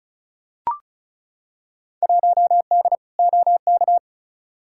Morse code, heard as a keyed tone, about 36 wpm, 700 Hz. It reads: 1D OK
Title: Field Day Exchanges - 35wpm